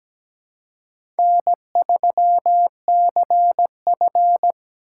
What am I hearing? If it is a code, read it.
Morse code, heard as a keyed tone, about 17 wpm, 700 Hz. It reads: N3CF
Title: USA Call Signs - 2x Character Spacing - 17wpm